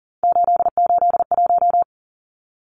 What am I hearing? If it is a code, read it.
Morse code, heard as a keyed tone, about 40 wpm, 700 Hz. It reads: 881
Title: Numbers - 3-Digit 40wpm